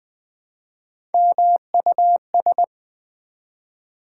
Morse code, heard as a keyed tone, about 20 wpm, 700 Hz. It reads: MUS
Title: Build A Word - 1 Letter At A Time - 3 to 5-Letter Words - 20wpm